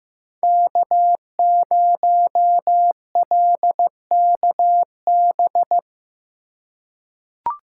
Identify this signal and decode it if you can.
Morse code, heard as a keyed tone, about 15 wpm, 700 Hz. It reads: K0LKB
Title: USA Call Signs - 2x Character Spacing - 15wpm